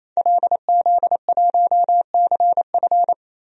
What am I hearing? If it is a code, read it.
Morse code, heard as a keyed tone, about 28 wpm, 700 Hz. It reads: LZ1CF
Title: DXCC-100 Call Signs - 5 Characters - 28wpm